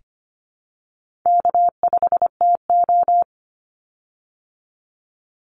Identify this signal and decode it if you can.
Morse code, heard as a keyed tone, about 25 wpm, 700 Hz. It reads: K5TO
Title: Call Signs Repeated 3x - 25wpm